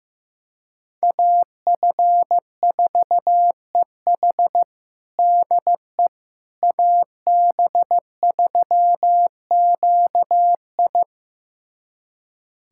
Morse code, heard as a keyed tone, about 15 wpm, 700 Hz. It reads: AF4EH DE AB3QI
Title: USA Callsign DE Callsign - 5 Characters - 15wpm